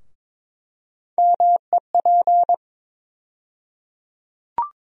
Morse code, heard as a keyed tone, about 22 wpm, 700 Hz. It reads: MEP
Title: USA Call Sign Postfixes - 3 Letters 22wpm